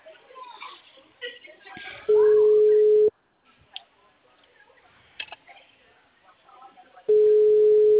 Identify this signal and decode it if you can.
Ringback tone (Europe, 425 Hz), with speech.